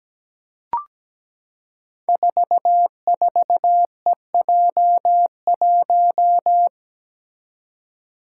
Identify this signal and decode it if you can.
Morse code, heard as a keyed tone, about 17 wpm, 700 Hz. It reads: 44EJ1
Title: Code Groups - 5 Letter:Numbers - 17wpm